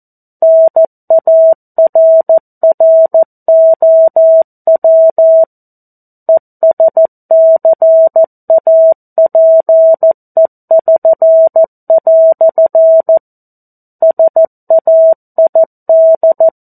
Morse code, heard as a keyed tone, about 14 wpm, 650 Hz. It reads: NARROW ESCAPE<SN>" SAID
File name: 02-Chapter-02 14 WPM